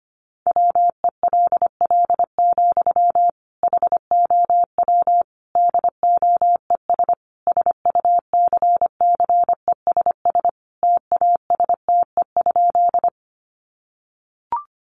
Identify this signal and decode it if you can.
Morse code, heard as a keyed tone, about 25 wpm, 700 Hz. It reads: WELL, HOW DOES SUCCESS TASTE?
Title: Sentences from Top 1100 Words 25wpm